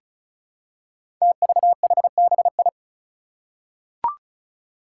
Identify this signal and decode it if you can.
Morse code, heard as a keyed tone, about 36 wpm, 700 Hz. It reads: TVHBI
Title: Five Challenging Letter-Number Combinations 35wpm